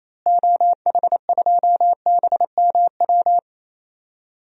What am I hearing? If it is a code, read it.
Morse code, heard as a keyed tone, about 27 wpm, 700 Hz. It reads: OH2BMW
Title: DXCC-100 Call Signs - 6 Characters - Repeated 2x - 28wpm